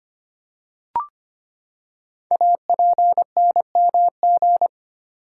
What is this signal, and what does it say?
Morse code, heard as a keyed tone, about 25 wpm, 690 Hz. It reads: APNMG
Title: Code Groups - 5 Letters - 25wpm